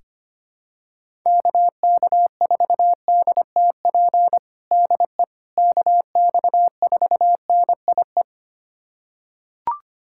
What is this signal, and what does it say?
Morse code, heard as a keyed tone, about 25 wpm, 700 Hz. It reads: KK4DTP DE KX4NIE
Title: USA Callsign DE Callsign - 6 Characters - 25wpm